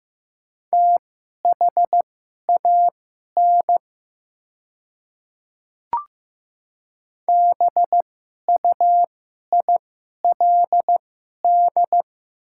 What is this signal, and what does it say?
Morse code, heard as a keyed tone, about 15 wpm, spaced as for 11 wpm, 700 Hz. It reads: THAN BUILD
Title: 2x Character Spacing - Top 200 Words - 15wpm